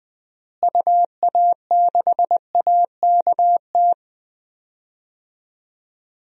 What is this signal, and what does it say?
Morse code, heard as a keyed tone, about 19 wpm, 700 Hz. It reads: UA6AKT
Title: DXCC-100 Call Signs - 6 Characters - Repeated 3x - 20wpm